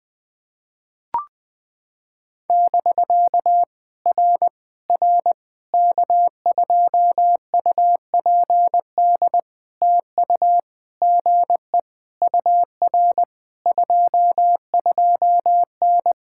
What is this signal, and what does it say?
Morse code, heard as a keyed tone, about 20 wpm, 700 Hz. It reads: <BK> R R K2UPD TU GE UR 22N
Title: POTA QSO Practice - 20wpm